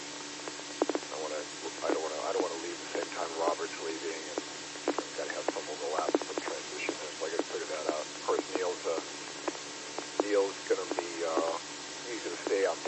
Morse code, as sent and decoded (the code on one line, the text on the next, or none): none